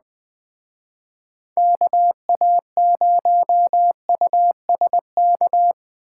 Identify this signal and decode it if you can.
Morse code, heard as a keyed tone, about 20 wpm, 700 Hz. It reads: KA0USK
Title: Speed Racing - USA 2x3 Call Signs 17wpm